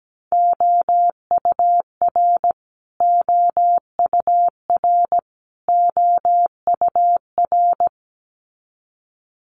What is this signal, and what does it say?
Morse code, heard as a keyed tone, about 17 wpm, 700 Hz. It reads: OUR OUR OUR
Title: Top 200 Words Repeated 17wpm